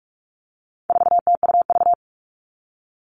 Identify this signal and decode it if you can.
Morse code, heard as a keyed tone, about 44 wpm, 700 Hz. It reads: 4TUV
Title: All Cut Numbers - 4-Digit 45wpm